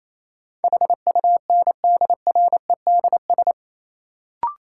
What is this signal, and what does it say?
Morse code, heard as a keyed tone, about 28 wpm, 700 Hz. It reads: HUNDREDS
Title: Sets of 4 Words - New Word Drill - 28wpm